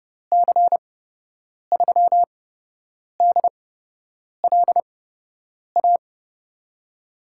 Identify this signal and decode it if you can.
Morse code, heard as a keyed tone, about 30 wpm, spaced as for 8 wpm, 700 Hz. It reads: C3DLA
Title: USA Call Signs - 8x Character Spacing - 30wpm